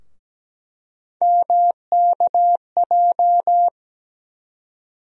Morse code, heard as a keyed tone, about 17 wpm, 700 Hz. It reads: MKJ
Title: USA Call Sign Postfixes - 3 Letters 17wpm